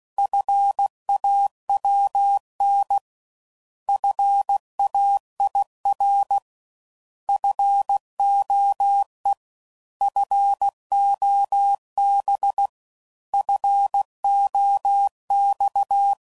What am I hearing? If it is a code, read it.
Morse code, heard as a keyed tone, about 16 wpm, 800 Hz. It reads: FAWN FAIR FOE FOB FOX